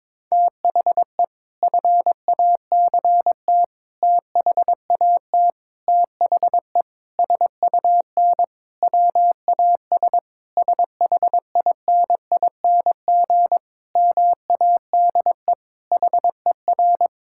Morse code, heard as a keyed tone, about 22 wpm, 700 Hz. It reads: THE FACT THAT THE SUN WAS SHINING MADE HER
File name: practice / 22wpm / Sentences with 4-Word Phrases - 22wpm